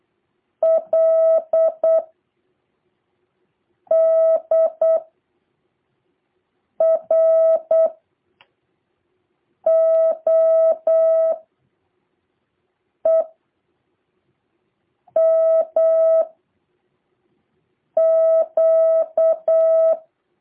Morse code, heard as a keyed tone, about 8 wpm, 650 Hz. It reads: L D R O E M Q
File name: jh8cbh / 2023cw